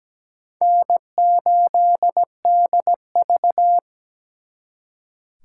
Morse code, heard as a keyed tone, about 20 wpm, 700 Hz. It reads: N8DV
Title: Speed Racing - USA 1x2 Call Signs 17wpm